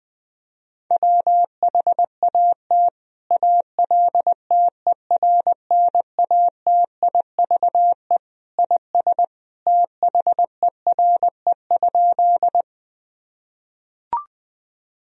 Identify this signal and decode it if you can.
Morse code, heard as a keyed tone, about 21 wpm, 700 Hz. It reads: WHAT ALTERNATIVE IS THERE?
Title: Sentences from Top 1000 Words - Review Part 4 - 20wpm